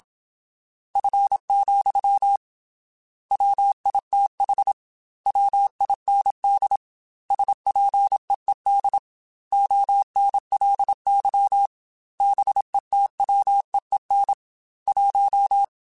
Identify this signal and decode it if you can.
Morse code, heard as a keyed tone, about 27 wpm, 800 Hz. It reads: F, WITH WIND SPEED ONLY BETWEEN 1
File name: Weather Sets / Weather Sets-mp3s / Weather 4 28 @25